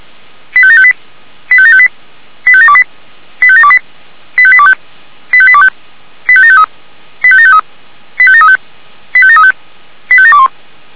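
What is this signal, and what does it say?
Five-tone selective call (ZVEI), heard as quick runs of five tones, tones of about 70 ms. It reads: 85758 85758 85728 85728 85825 85825 85753 85753 85735 85735 85721
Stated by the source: ZVEI five-tone call 85730 received by SDR, logged by Feuerwehr Orlamünde.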